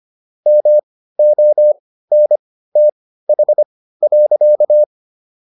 Morse code, heard as a keyed tone, about 25 wpm, spaced as for 15 wpm, 600 Hz. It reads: MONTH.